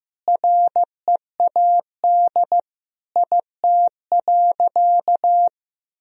Morse code, heard as a keyed tone, about 15 wpm, 700 Hz. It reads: READ IT.
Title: Sentences from Top 500 Words and Abbreviations 15wpm